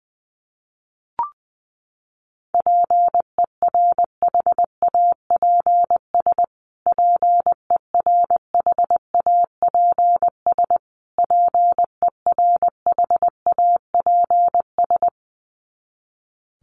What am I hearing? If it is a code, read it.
Morse code, heard as a keyed tone, about 19 wpm, 700 Hz. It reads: PERHAPS PERHAPS PERHAPS
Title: Top 500 Words Repeated 20wpm